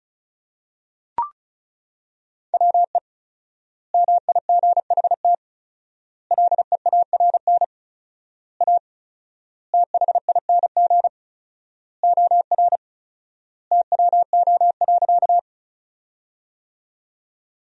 Morse code, heard as a keyed tone, about 34 wpm, 700 Hz. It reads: WE MIGHT LEARN A THING OR TWO.
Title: 4x Word-Spacing - Sentences from Top 300 Words - 35wpm